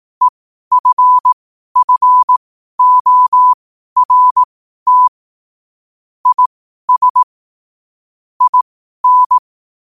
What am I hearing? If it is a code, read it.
Morse code, heard as a keyed tone, about 18 wpm, spaced as for 12 wpm, 1000 Hz. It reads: EFFORT IS IN
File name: qotdfiles / 12wpm